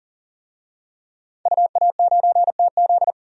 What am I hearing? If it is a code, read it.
Morse code, heard as a keyed tone, about 40 wpm, 700 Hz. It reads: UA9TZ
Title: DXCC-100 Call Signs - 5 Characters - 40wpm